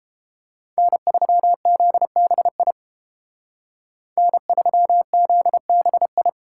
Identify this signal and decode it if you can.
Morse code, heard as a keyed tone, about 29 wpm, 700 Hz. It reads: N3ZBI N3ZBI